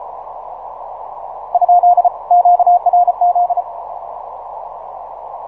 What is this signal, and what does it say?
Morse code, heard as a keyed tone, about 34 wpm, 710 Hz. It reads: ? QRZ
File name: simple-file-list / 5A1AL 6MCW